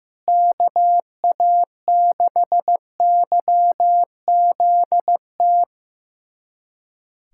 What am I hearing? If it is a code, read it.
Morse code, heard as a keyed tone, about 15 wpm, 700 Hz. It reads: KA6YZT
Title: USA Callsign DE Callsign - 6 Characters - 15wpm